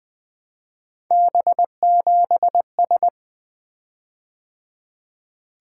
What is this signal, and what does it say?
Morse code, heard as a keyed tone, about 20 wpm, 700 Hz. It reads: B7S
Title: Three Letter-Number Combinations - 20wpm